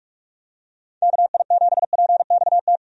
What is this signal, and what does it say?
Morse code, heard as a keyed tone, about 45 wpm, 700 Hz. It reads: KI7PXT
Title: USA Call Signs - 6 Characters - 45wpm